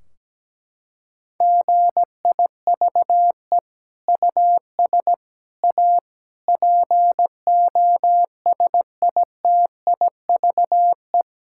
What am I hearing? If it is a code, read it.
Morse code, heard as a keyed tone, about 17 wpm, 700 Hz. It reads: GIVE US A POSITIVE
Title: Sentences from Top 1600 Words 17wpm